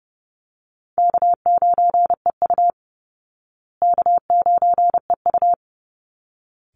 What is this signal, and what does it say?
Morse code, heard as a keyed tone, about 30 wpm, 700 Hz. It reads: K9EU K9EU
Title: Call Signs Repeated 3x - 30wpm